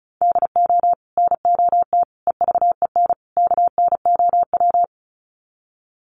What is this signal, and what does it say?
Morse code, heard as a keyed tone, about 34 wpm, 700 Hz. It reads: DO NOT EVEN KNOW
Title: Sets of 4 Words 35wpm